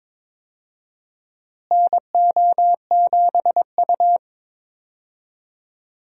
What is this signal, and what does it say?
Morse code, heard as a keyed tone, about 22 wpm, 700 Hz. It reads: NO7U
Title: USA Call Signs - 4 Characters - Repeated 2x - 22wpm